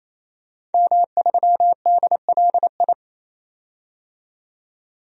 Morse code, heard as a keyed tone, about 27 wpm, 700 Hz. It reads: M3DLI
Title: DXCC-100 Call Signs - 5 Characters - 28wpm